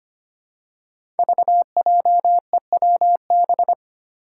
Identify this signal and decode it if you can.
Morse code, heard as a keyed tone, about 25 wpm, 700 Hz. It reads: VJEWB
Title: Five Challenging Letter-Number Combinations 25wpm